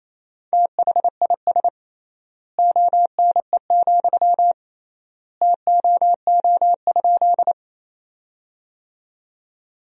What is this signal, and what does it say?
Morse code, heard as a keyed tone, about 28 wpm, 700 Hz. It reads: THIS ONE, TOO?
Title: 3x Word-Spacing - Sentences from Top 200 Words - 28wpm